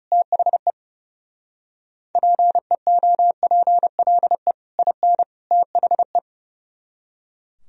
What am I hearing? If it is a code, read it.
Morse code, heard as a keyed tone, about 37 wpm, 700 Hz. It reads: THE PEOPLE IN THE
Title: Speed-Racing - Sets of 3 Words 30wpm